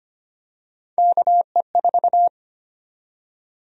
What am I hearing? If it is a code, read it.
Morse code, heard as a keyed tone, about 25 wpm, 700 Hz. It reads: KE4
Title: USA Call Sign Prefixes - 25wpm